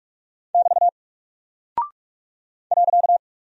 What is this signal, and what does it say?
Morse code, heard as a keyed tone, about 45 wpm, 700 Hz. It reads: <BT> .
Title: Single Character (Letter:Number:Punctuation:Prosign) Alt 45wpm